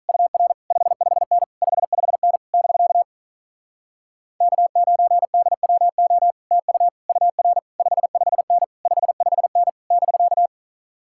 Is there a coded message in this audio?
Yes: Morse code from 0.1 to 10.5 s, heard as a keyed tone, about 48 wpm, 700 Hz. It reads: UR 55N 55N <BK> K9DWO TU UR 55N 55N <BK>